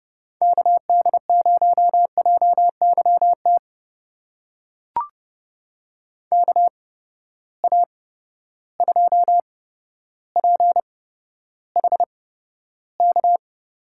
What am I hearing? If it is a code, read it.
Morse code, heard as a keyed tone, about 30 wpm, spaced as for 13 wpm, 700 Hz. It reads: KD0JYT KA2PHK
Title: USA Call Signs - 8x Character Spacing - 30wpm